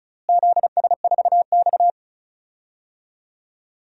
Morse code, heard as a keyed tone, about 34 wpm, 700 Hz. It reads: ZS4X
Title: DXCC-100 Call Signs - 4 Characters - Repeated 3x - 35wpm